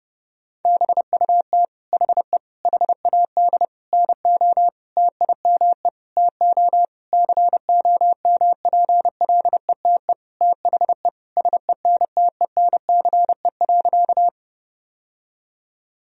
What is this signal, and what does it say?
Morse code, heard as a keyed tone, about 30 wpm, 700 Hz. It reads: BUT HE HAD NO TIME TO COMPLETE THE SENTENCE.
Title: Sentences from Top 1000 Words - Review Part 1 - 30wpm